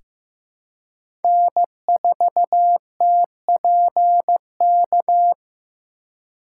Morse code, heard as a keyed tone, about 15 wpm, 700 Hz. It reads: N4TPK
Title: USA Call Signs - 5 Characters - 3x Character Spacing - 15wpm